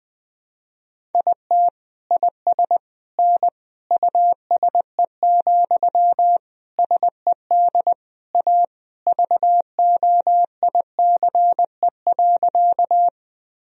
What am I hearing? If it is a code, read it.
Morse code, heard as a keyed tone, about 21 wpm, 700 Hz. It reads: IT IS N USE, SED A VOICE.
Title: Sentences from Top 500 Words and Abbreviations 20wpm